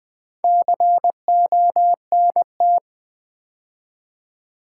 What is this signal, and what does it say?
Morse code, heard as a keyed tone, about 20 wpm, 700 Hz. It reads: CONT
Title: Build A Word - 2 Letters At A Time - 8-Letter Words - 20wpm